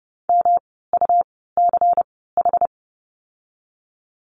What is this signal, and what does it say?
Morse code, heard as a keyed tone, about 30 wpm, spaced as for 17 wpm, 700 Hz. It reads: MUCH
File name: course / 30:10wpm / 108 Lesson 18f - Letter m - Words 30wpm